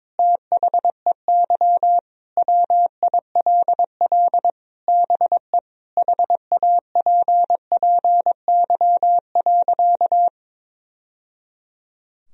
Morse code, heard as a keyed tone, about 22 wpm, 700 Hz. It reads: THEY WILL BE HAPPY.
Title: Sentences from Top 1000 Words - Review Part 2 - 22wpm